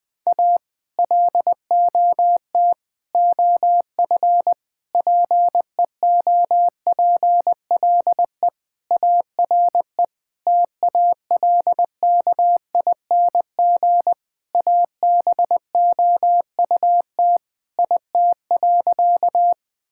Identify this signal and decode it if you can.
Morse code, heard as a keyed tone, about 20 wpm, 700 Hz. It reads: A LOT OF PEOPLE ARE TALKING ABOUT IT.
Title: Sentences with 5-Word Phrases - 20wpm